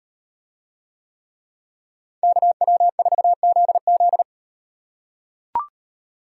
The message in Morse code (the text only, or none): KW4ZZ